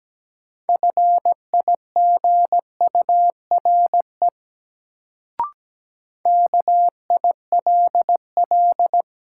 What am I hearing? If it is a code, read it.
Morse code, heard as a keyed tone, about 17 wpm, 700 Hz. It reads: FIGURE KILL
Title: Sentences from Top 1400 Words - New Word Drill Alt - 17wpm